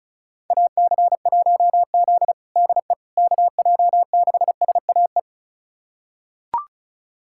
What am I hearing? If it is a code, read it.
Morse code, heard as a keyed tone, about 36 wpm, 700 Hz. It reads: AC1Z DE KJ6SAE
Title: USA Callsign DE Callsign 35wpm